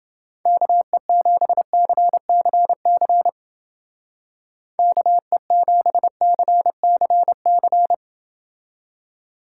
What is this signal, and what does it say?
Morse code, heard as a keyed tone, about 28 wpm, 700 Hz. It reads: KE7CCC KE7CCC